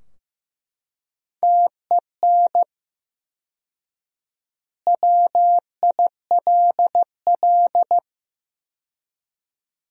Morse code, heard as a keyed tone, about 15 wpm, 700 Hz. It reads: TEN WILL
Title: 4x Word-Spacing - Sentences from Top 400 Words - 15wpm